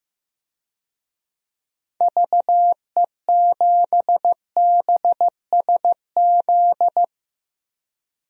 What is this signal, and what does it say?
Morse code, heard as a keyed tone, about 15 wpm, 700 Hz. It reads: VE7BSZ
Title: DXCC-100 Call Signs - 6 Characters - 15wpm